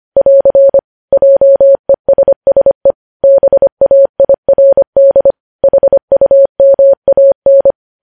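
Morse code, heard as a keyed tone, about 25 wpm, 550 Hz. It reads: <AR> JESSE BAIRD HUMAN
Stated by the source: Morse code news headlines at 25 wpm.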